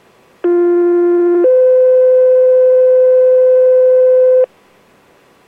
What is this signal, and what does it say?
Two-tone page, heard as a one-second tone, then a three-second tone, about 350 then 510 Hz. It is heard from 0.4 to 4.5 s.